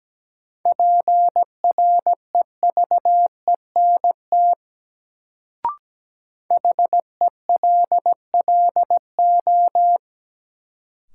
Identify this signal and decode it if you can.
Morse code, heard as a keyed tone, about 17 wpm, 700 Hz. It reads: PREVENT HELLO
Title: Sentences from Top 1100 Words - New Word Drill Alt - 17wpm